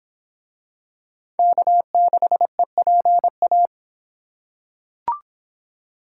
Morse code, heard as a keyed tone, about 22 wpm, 700 Hz. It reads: K6EPA